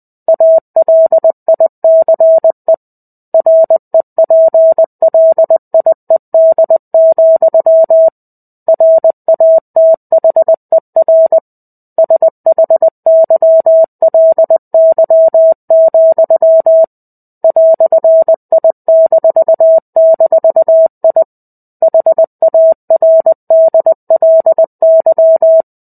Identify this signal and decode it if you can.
Morse code, heard as a keyed tone, about 20 wpm, 650 Hz. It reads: ALICE REPLIED, RATHER SHYLY, "I--I HARDLY
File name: 05-Chapter-05 20 WPM